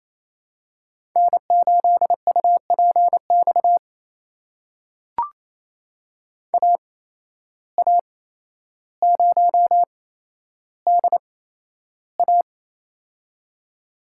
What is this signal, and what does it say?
Morse code, heard as a keyed tone, about 28 wpm, spaced as for 12 wpm, 700 Hz. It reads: N8UPX AA0DA
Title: USA Call Signs - 5 Characters - 8x Character Spacing - 28wpm